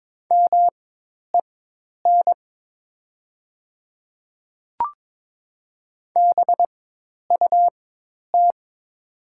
Morse code, heard as a keyed tone, about 22 wpm, spaced as for 10 wpm, 700 Hz. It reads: MEN BUT
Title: 4x Character Spacing - Top 200 Words - 22wpm